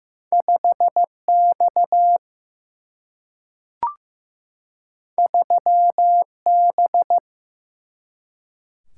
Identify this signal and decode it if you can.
Morse code, heard as a keyed tone, about 15 wpm, 700 Hz. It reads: 5 X 3 B
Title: Single-Letter Single-Number 15wpm